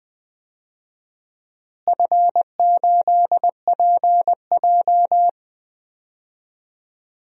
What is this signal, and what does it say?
Morse code, heard as a keyed tone, about 20 wpm, 700 Hz. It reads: F8PJ
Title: DXCC-100 Call Signs - 4 Characters - Repeated 2x - 20wpm